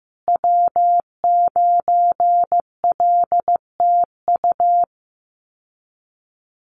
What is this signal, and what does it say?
Morse code, heard as a keyed tone, about 15 wpm, 700 Hz. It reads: W9LTU
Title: USA Call Signs Repeated 3x 15wpm